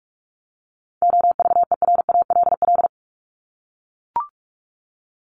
Morse code, heard as a keyed tone, about 44 wpm, 700 Hz. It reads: OVERALL
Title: Top 751-1000 Words 45wpm